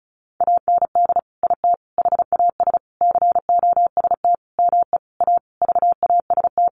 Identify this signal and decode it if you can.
Morse code, heard as a keyed tone, about 34 wpm, 700 Hz. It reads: AND IT HAS COST ME A VAST
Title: Sentences from Top 1300 Words 35wpm